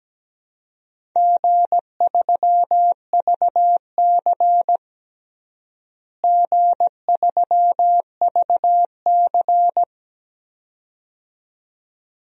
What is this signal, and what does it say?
Morse code, heard as a keyed tone, about 17 wpm, 700 Hz. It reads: G3VC G3VC
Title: DXCC-100 Call Signs - 4 Characters - Repeated 2x - 17wpm